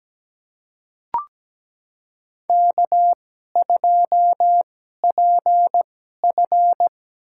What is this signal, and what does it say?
Morse code, heard as a keyed tone, about 17 wpm, spaced as for 12 wpm, 700 Hz. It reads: K2PF
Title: USA Call Signs - 4 Characters - 2x Character Spacing - 17wpm